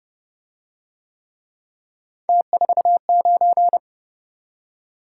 Morse code, heard as a keyed tone, about 27 wpm, 700 Hz. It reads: T49